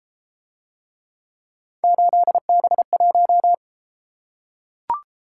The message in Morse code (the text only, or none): 8B1